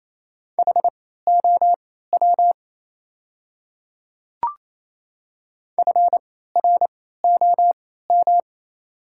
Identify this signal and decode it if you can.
Morse code, heard as a keyed tone, about 28 wpm, spaced as for 16 wpm, 690 Hz. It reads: HOW FROM
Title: 3x Character Spacing - Top 100 Words - 28wpm